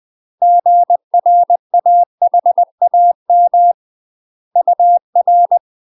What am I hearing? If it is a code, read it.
Morse code, heard as a keyed tone, about 20 wpm, 700 Hz. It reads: GRAHAM UR
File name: UK MASTER QSOs TRAINING/20WPM